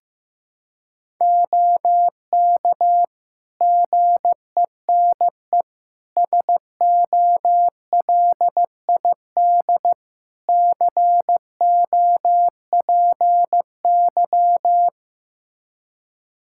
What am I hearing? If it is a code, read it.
Morse code, heard as a keyed tone, about 15 wpm, 700 Hz. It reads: OK GENE SOLID COPY
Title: Practice QSO - Exchange 3 - More Info - 15wpm